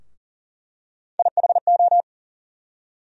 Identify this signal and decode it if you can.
Morse code, heard as a keyed tone, about 38 wpm, 700 Hz. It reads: IHO
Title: USA Call Sign Postfixes - 3 Letters 40wpm